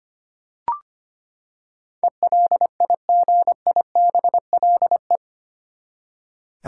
Morse code, heard as a keyed tone, about 25 wpm, 700 Hz. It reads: ELIGIBLE
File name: Eight Letter Words - 25wpm